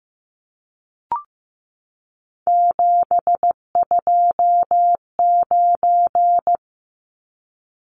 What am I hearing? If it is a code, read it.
Morse code, heard as a keyed tone, about 15 wpm, 700 Hz. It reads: 729
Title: Numbers - 3-Digit 15wpm